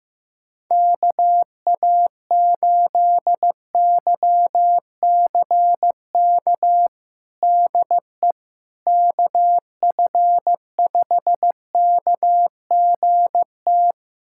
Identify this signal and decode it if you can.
Morse code, heard as a keyed tone, about 16 wpm, 700 Hz. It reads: KA8YCK DE KF5KGT
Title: USA Callsign DE Callsign - 6 Characters - 15wpm